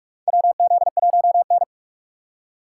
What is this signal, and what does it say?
Morse code, heard as a keyed tone, about 42 wpm, 700 Hz. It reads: WZ1N
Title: USA Call Signs - 4 Characters - 3x Character Spacing - 45wpm